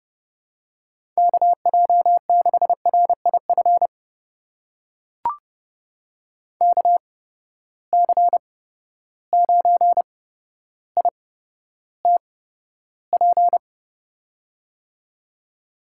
Morse code, heard as a keyed tone, about 32 wpm, spaced as for 13 wpm, 700 Hz. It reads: KJ6RIF KC9ITP
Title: USA Call Signs - 6 Characters - 8x Character Spacing - 30wpm